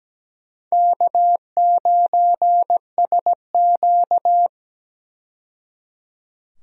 Morse code, heard as a keyed tone, about 17 wpm, 700 Hz. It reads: K9SQ
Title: USA Call Signs - 4 Characters - 17wpm